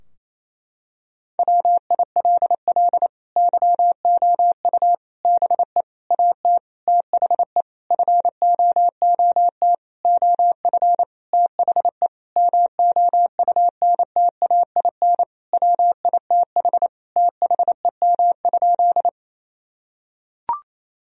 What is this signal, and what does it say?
Morse code, heard as a keyed tone, about 28 wpm, 700 Hz. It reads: WILL YOU BE AT THE FOOT OF THE MOUNTAIN WITH THEM?
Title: Sentences from Top 1000 Words - Review Part 3 - 28wpm